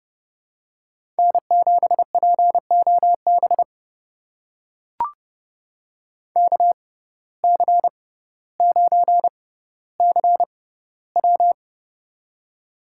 Morse code, heard as a keyed tone, about 30 wpm, spaced as for 15 wpm, 690 Hz. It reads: N7POB KC9CW
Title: USA Call Signs - 5 Characters - 6x Character Spacing - 30wpm